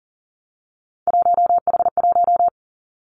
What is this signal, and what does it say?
Morse code, heard as a keyed tone, about 41 wpm, 700 Hz. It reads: 1H1
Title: Three Challenging Letter-Number Combinations 40wpm